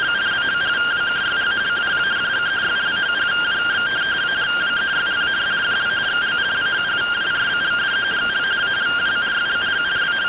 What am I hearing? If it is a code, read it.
Two-tone teleprinter signal, about 170 Hz shift.
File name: findings / 202 coronal iodine 3mm.gen-04 rtty rx